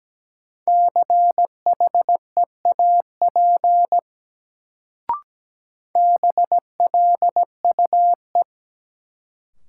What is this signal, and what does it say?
Morse code, heard as a keyed tone, about 17 wpm, 700 Hz. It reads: CHEAP BLUE
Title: Sentences from Top 1200 Words - New Word Drill Alt - 17wpm